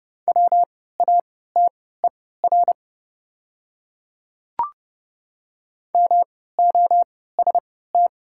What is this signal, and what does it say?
Morse code, heard as a keyed tone, about 30 wpm, spaced as for 17 wpm, 700 Hz. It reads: WATER MOST
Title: 3x Character Spacing - Top 100 Words - 30wpm